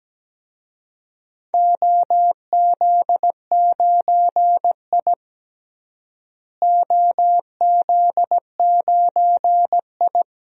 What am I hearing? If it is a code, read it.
Morse code, heard as a keyed tone, about 17 wpm, 700 Hz. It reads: OZ9I OZ9I
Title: DXCC-100 Call Signs - 4 Characters - Repeated 2x - 17wpm